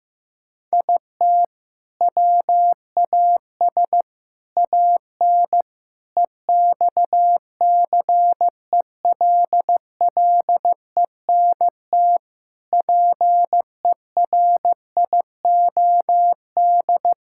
Morse code, heard as a keyed tone, about 15 wpm, 700 Hz. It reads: IT WAS AN EXCELLENT PERIOD
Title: Sentences from Top 1000 Words - Review Part 5 - 15wpm